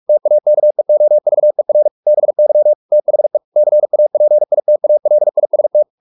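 Morse code, heard as a keyed tone, about 44 wpm, 600 Hz. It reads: TAKEOVER BY THE CAPITALIST
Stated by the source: Morse practice at 45 wpm, the novel Never by Ken Follett in English, from CW med Gnister.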